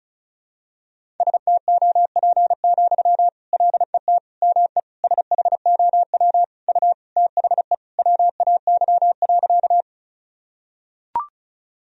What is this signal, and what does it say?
Morse code, heard as a keyed tone, about 34 wpm, 700 Hz. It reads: STOP, LET ME SHOW U THE WAY.
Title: Sentences from Top 300 Words and Abbreviations 35wpm